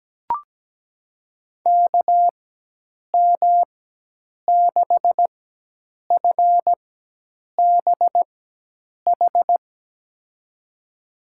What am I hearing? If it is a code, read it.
Morse code, heard as a keyed tone, about 17 wpm, spaced as for 8 wpm, 700 Hz. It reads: KM6FBH
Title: USA Call Signs - 4x Character Spacing - 17wpm